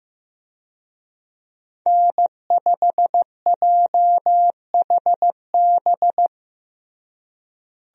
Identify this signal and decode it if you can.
Morse code, heard as a keyed tone, about 16 wpm, 700 Hz. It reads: N5JHB
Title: USA Call Signs - 5 Characters - Repeated 3x - 15wpm